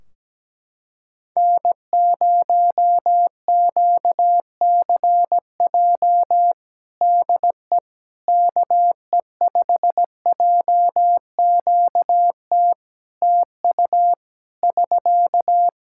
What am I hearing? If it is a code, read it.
Morse code, heard as a keyed tone, about 17 wpm, 700 Hz. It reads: N0QCJ DE KE5JQT TU <SK>
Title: Practice QSO - Exchange 4 - Ending - 17wpm